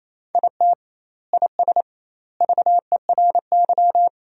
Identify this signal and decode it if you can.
Morse code, heard as a keyed tone, about 28 wpm, 700 Hz. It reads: IT IS VERY
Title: 2x Word-Spacing - Sentences from Top 500 Words - 28wpm